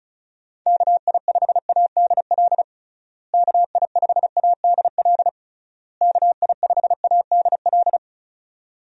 Morse code, heard as a keyed tone, about 36 wpm, 700 Hz. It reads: KI5ADL KI5ADL KI5ADL
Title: USA Call Signs - 6 Characters - Repeated 3x - 35wpm